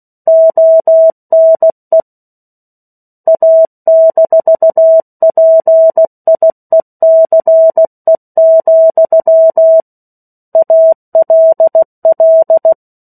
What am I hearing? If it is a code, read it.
Morse code, heard as a keyed tone, about 16 wpm, 650 Hz. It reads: ONE A-PIECE, ALL